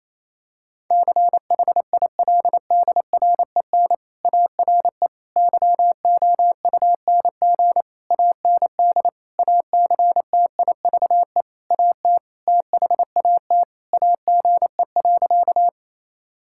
Morse code, heard as a keyed tone, about 27 wpm, 700 Hz. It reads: CHILDREN ARE YOUNG AND ACTIVE AT THAT AGE.
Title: Sentences from Top 1000 Words - Review Part 6 - 28wpm